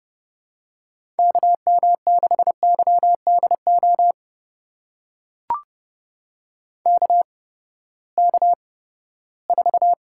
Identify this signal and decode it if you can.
Morse code, heard as a keyed tone, about 30 wpm, spaced as for 17 wpm, 700 Hz. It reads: KM6YDO KK4
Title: USA Call Signs - 6 Characters - 8x Character Spacing - 30wpm